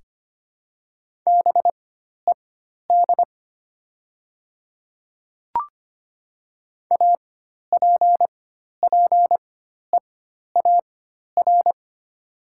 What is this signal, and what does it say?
Morse code, heard as a keyed tone, about 25 wpm, spaced as for 12 wpm, 700 Hz. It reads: BED APPEAR
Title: 4x Character Spacing - Top 500 Words - 25wpm